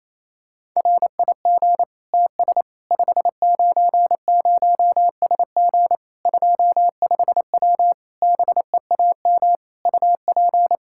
Morse code, heard as a keyed tone, about 28 wpm, 700 Hz. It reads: RIG TS 590SG 25W BEAM UP
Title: Practice QSO - Exchange 2 - Info - 28wpm